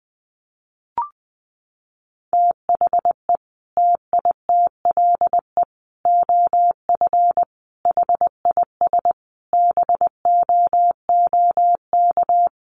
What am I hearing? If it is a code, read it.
Morse code, heard as a keyed tone, about 20 wpm, 700 Hz. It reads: THE TITLE OF HIS BOOK
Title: Sentences from Top 1200 Words 20wpm